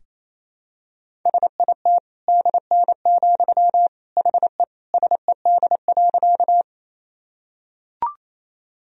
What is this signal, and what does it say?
Morse code, heard as a keyed tone, about 27 wpm, 700 Hz. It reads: SIT DN, HE SED.
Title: Sentences from Top 500 Words and Abbreviations 28wpm